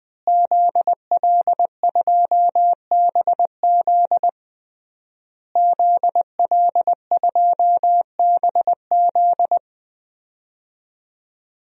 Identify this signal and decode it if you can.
Morse code, heard as a keyed tone, about 20 wpm, 700 Hz. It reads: ZL2BZ ZL2BZ
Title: DXCC-100 Call Signs - 5 Characters - Repeated 3x - 20wpm